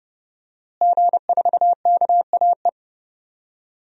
Morse code, heard as a keyed tone, about 30 wpm, 700 Hz. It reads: G4KAE
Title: DXCC-100 Call Signs - 5 Characters - Repeated 2x - 30wpm